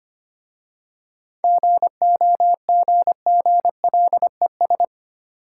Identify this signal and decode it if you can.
Morse code, heard as a keyed tone, about 28 wpm, 700 Hz. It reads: GOGGLES
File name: Easy Words - Speed-Racing with Hints - 25wpm